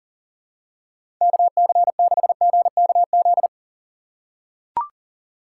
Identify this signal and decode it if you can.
Morse code, heard as a keyed tone, about 38 wpm, 700 Hz. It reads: KC6GKZ
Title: USA Call Signs - 6 Characters - 3x Character Spacing - 40wpm